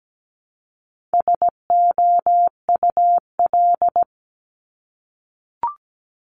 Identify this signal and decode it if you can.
Morse code, heard as a keyed tone, about 17 wpm, 700 Hz. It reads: SOUL
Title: Multi-letter Phonemes Repeated 3x 17wpm